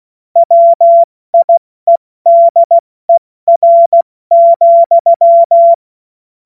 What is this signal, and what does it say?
Morse code, heard as a keyed tone, about 16 wpm, spaced as for 14 wpm, 680 Hz. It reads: WIEDER,
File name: Morsekurs / Witze / Witz 30. Mai 2023 70 F80